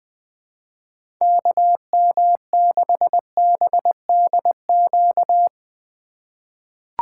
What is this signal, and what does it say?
Morse code, heard as a keyed tone, about 21 wpm, 700 Hz. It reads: KM6BDQ
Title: USA Call Signs - 6 Characters - 3x Character Spacing - 20wpm